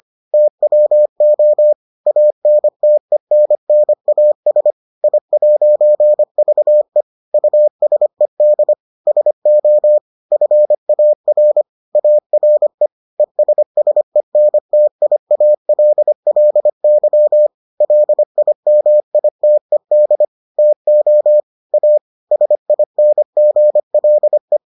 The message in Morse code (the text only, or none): TWO ANTENNAS I'VE USED SO FAR ARE ESSENTIALLY LIMITED TO A SINGLE